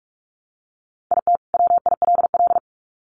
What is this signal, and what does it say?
Morse code, heard as a keyed tone, about 44 wpm, 700 Hz. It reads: IT WILL
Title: Sets of 2 Words 45wpm